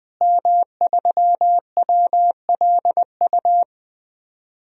Morse code, heard as a keyed tone, about 20 wpm, 700 Hz. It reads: M3WLU
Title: DXCC-100 Call Signs - 5 Characters - Repeated 3x - 20wpm